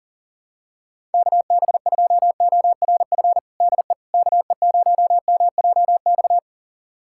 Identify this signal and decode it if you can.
Morse code, heard as a keyed tone, about 40 wpm, 700 Hz. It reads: KB2ORF DE KE0MJX
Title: USA Callsign DE Callsign - 6 Characters - 40wpm